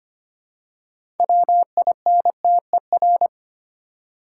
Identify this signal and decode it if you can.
Morse code, heard as a keyed tone, about 25 wpm, 700 Hz. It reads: WINTER
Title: Six Letter Words - 25wpm